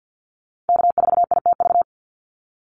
Morse code, heard as a keyed tone, about 48 wpm, 700 Hz. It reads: K4ITV
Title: Call Signs Repeated 3x - 50wpm